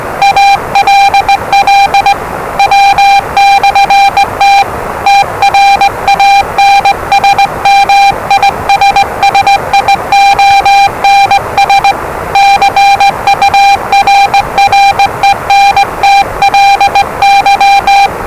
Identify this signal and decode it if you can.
Morse code, heard as a keyed tone, about 20 wpm, 800 Hz, with noise. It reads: ALL W/T TRANSMISSIONS CURRENTLY
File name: WellingtonRadioZlwCloseDownOnHfAndMf / 01-Zlw 4277khz 417.5 qrt until 0925z